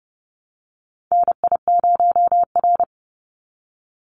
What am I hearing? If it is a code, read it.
Morse code, heard as a keyed tone, about 30 wpm, 700 Hz. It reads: NI0R
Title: Call Signs - 30wpm